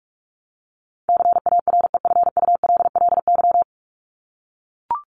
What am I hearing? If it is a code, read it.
Morse code, heard as a keyed tone, about 44 wpm, 700 Hz. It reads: CAREFULLY
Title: LY Suffix - Top 100 Words with Base Words 45wpm